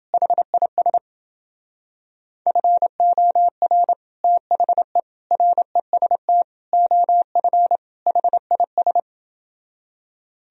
Morse code, heard as a keyed tone, about 29 wpm, 700 Hz. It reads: HIS FOR THE REST OF HIS